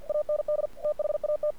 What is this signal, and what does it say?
Morse code, heard as a keyed tone, about 47 wpm, 600 Hz. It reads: AND THAT
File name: archived / CRSnet